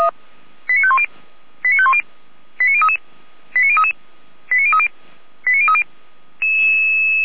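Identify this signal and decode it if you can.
Five-tone selective call (ZVEI), heard as quick runs of five tones, tones of about 70 ms. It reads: 89510 89510 89033 89033 89030 89030
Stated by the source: ZVEI five-tone selective call received with an SDR, Feuerwehr Orlamünde.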